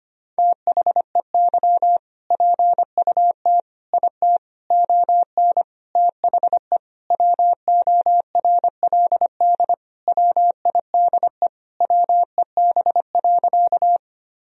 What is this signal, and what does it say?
Morse code, heard as a keyed tone, about 25 wpm, 700 Hz. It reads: THEY PUT IT ON THE WORLD WIDE WEB.
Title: Sentences from Top 1000 Words - Review Part 4 - 25wpm